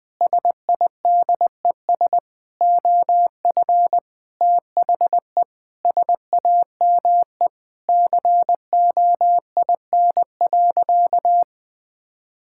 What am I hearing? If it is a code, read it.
Morse code, heard as a keyed tone, about 20 wpm, 700 Hz. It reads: SIDES OF THE SAME COIN.
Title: Sentences with 7-Word Phrases - 20wpm